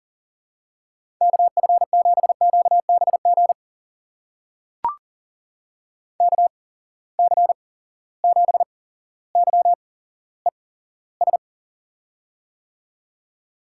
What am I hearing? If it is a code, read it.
Morse code, heard as a keyed tone, about 42 wpm, spaced as for 17 wpm, 700 Hz. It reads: KF7QBG KC7YES
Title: USA Call Signs - 6 Characters - 8x Character Spacing - 40wpm